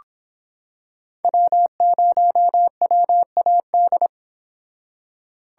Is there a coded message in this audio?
Yes: Morse code from 1.2 to 4.1 s, heard as a keyed tone, about 26 wpm, 700 Hz. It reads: W0WAD